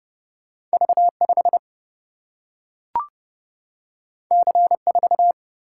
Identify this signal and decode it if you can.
Morse code, heard as a keyed tone, about 30 wpm, 700 Hz. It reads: V5 C4
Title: Single-Letter Single-Number 30wpm